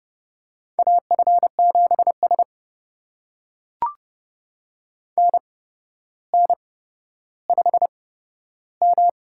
Morse code, heard as a keyed tone, about 32 wpm, spaced as for 13 wpm, 700 Hz. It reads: AF7S NN5M
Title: USA Call Signs - 4 Characters - 8x Character Spacing - 30wpm